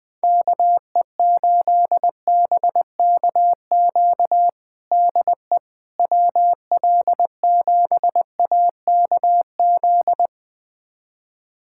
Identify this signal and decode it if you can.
Morse code, heard as a keyed tone, about 21 wpm, 700 Hz. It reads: KE8BKQ DE WL7AKZ
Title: USA Callsign DE Callsign - 6 Characters - 20wpm